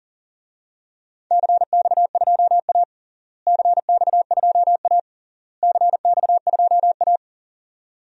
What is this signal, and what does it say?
Morse code, heard as a keyed tone, about 37 wpm, 700 Hz. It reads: CX2A CX2A CX2A
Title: DXCC-100 Call Signs - 4 Characters - Repeated 3x - 40wpm